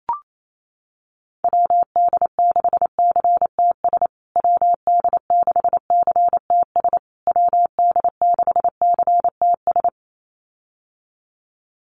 Morse code, heard as a keyed tone, about 28 wpm, 700 Hz. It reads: WD6CTS WD6CTS WD6CTS
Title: USA Call Signs Repeated 3x 28wpm